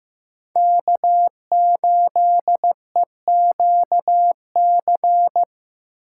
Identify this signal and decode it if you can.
Morse code, heard as a keyed tone, about 15 wpm, 700 Hz. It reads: K8EQC
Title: USA Call Signs - 5 Characters - 8x Character Spacing - 15wpm